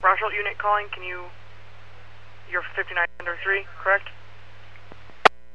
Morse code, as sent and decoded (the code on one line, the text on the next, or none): none